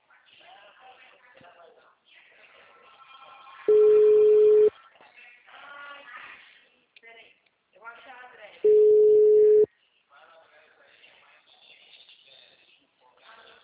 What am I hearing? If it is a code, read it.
Ringback tone (Europe, 425 Hz), with speech.
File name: out- -2039- - - .72154